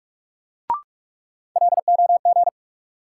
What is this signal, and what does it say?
Morse code, heard as a keyed tone, about 43 wpm, 700 Hz. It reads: LOG
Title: Sentences from Top 1400 Words - New Word Drill - 45wpm